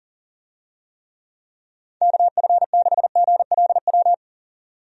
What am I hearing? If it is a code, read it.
Morse code, heard as a keyed tone, about 37 wpm, 700 Hz. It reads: KF6GLW
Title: USA Call Signs - 6 Characters - 40wpm